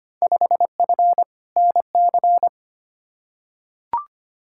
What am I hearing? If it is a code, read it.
Morse code, heard as a keyed tone, about 25 wpm, 700 Hz. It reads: 5F NC
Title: Field Day Exchanges Encore - 25wpm